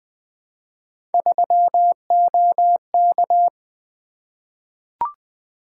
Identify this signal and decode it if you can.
Morse code, heard as a keyed tone, about 20 wpm, 700 Hz. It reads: 3OK
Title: Three Letter-Number Combinations - 20wpm